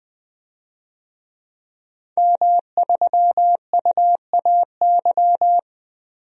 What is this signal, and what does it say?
Morse code, heard as a keyed tone, about 20 wpm, 700 Hz. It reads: M3UAY
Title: DXCC-100 Call Signs - 5 Characters - 20wpm